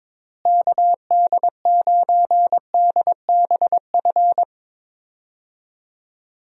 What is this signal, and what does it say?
Morse code, heard as a keyed tone, about 22 wpm, 700 Hz. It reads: KD9DBF
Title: USA Call Signs - 6 Characters - Repeated 2x - 22wpm